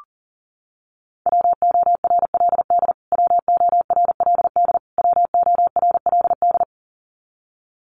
Morse code, heard as a keyed tone, about 40 wpm, 700 Hz. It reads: WORLD WORLD WORLD
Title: Top 500 Words Repeated 40wpm